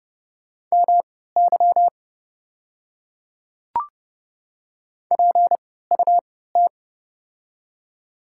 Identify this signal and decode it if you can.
Morse code, heard as a keyed tone, about 28 wpm, spaced as for 17 wpm, 690 Hz. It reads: MY PUT
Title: 3x Character Spacing - Top 300 Words - 30wpm